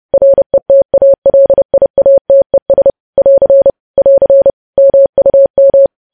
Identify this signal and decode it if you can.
Morse code, heard as a keyed tone, about 30 wpm, 550 Hz. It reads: RETALIATES <AR> <AR> MUM
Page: news headlines in morse code at 30 wpm